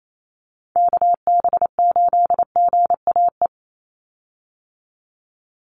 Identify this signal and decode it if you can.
Morse code, heard as a keyed tone, about 28 wpm, 700 Hz. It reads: KB8GAE
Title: Call Signs Repeated 3x - 28wpm